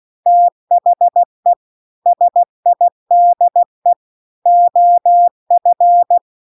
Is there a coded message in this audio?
Yes: Morse code from 0.3 to 6.2 s, heard as a keyed tone, about 16 wpm, 700 Hz. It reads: THE SIDE OF